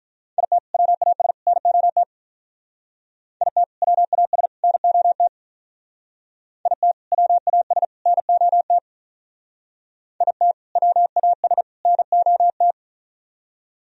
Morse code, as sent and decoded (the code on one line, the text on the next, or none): .. - / .-- .- ... / -. --- - / .. - / .-- .- ... / -. --- - / .. - / .-- .- ... / -. --- - / .. - / .-- .- ... / -. --- -
IT WAS NOT IT WAS NOT IT WAS NOT IT WAS NOT